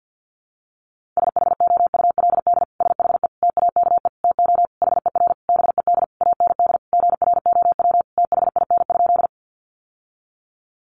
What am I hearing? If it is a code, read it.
Morse code, heard as a keyed tone, about 49 wpm, 700 Hz. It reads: SHOULD SHE TAKE TO HER BED AND GROW THIN?
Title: Sentences from Top 1200 Words 50wpm